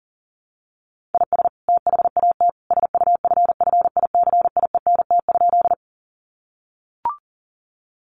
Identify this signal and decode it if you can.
Morse code, heard as a keyed tone, about 40 wpm, 700 Hz. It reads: IS THAT SUFFICIENT?
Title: Sentences from Top 1400 Words 40wpm